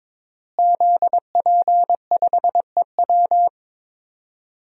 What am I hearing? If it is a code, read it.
Morse code, heard as a keyed tone, about 22 wpm, 700 Hz. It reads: ZP5EW
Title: DXCC-100 Call Signs - 5 Characters - Repeated 2x - 22wpm